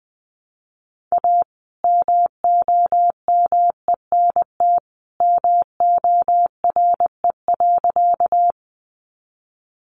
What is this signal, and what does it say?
Morse code, heard as a keyed tone, about 19 wpm, 700 Hz. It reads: A MOMENT MORE.
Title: Sentences from Top 1000 Words 20wpm